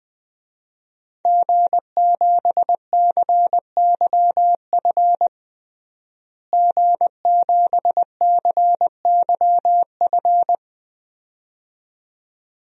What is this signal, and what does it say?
Morse code, heard as a keyed tone, about 20 wpm, 700 Hz. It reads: G7CYF G7CYF
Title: DXCC-100 Call Signs - 5 Characters - Repeated 2x - 20wpm